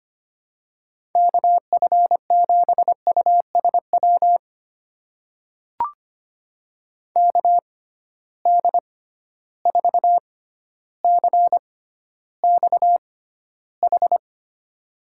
Morse code, heard as a keyed tone, about 25 wpm, spaced as for 13 wpm, 700 Hz. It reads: KF7USW KD4CXH
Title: USA Call Signs - 6 Characters - 6x Character Spacing - 25wpm